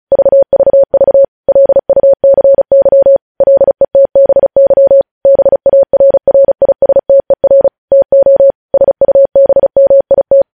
Morse code, heard as a keyed tone, about 36 wpm, 550 Hz. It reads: VVV LUCY LETBY BARRISTER TO SUBMIT
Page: news headlines in morse code at 35 wpm